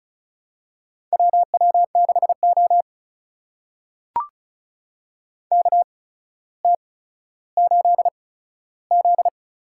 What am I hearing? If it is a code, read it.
Morse code, heard as a keyed tone, about 36 wpm, spaced as for 15 wpm, 700 Hz. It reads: WW6O KT8Z
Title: USA Call Signs - 4 Characters - 8x Character Spacing - 35wpm